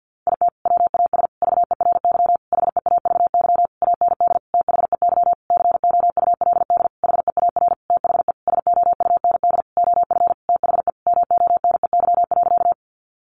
Morse code, heard as a keyed tone, about 50 wpm, 700 Hz. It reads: IT WAS VERY HEAVY AND THEY COULD HEAR THE SOUND OF THE MONEY.
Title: Sentences from Top 1100 Words 50wpm